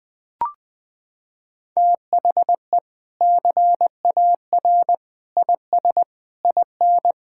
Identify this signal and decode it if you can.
Morse code, heard as a keyed tone, about 20 wpm, 700 Hz. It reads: THE CAR IS IN
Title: Sentences from Top 1600 Words 20wpm